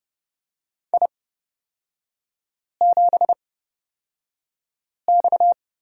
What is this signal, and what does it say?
Morse code, heard as a keyed tone, about 29 wpm, spaced as for 5 wpm, 690 Hz. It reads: I7X
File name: Lesson 37e - Letter x - Three Characters 30wpm